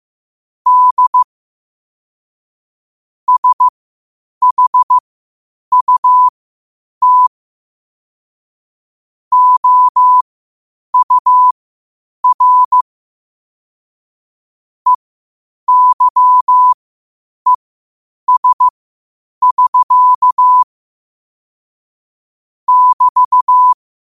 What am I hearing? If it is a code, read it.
Morse code, heard as a keyed tone, about 15 wpm, spaced as for 8 wpm, 1000 Hz. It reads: D SHUT OUR EYES <SK> <BT>
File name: qotdfiles / 8wpm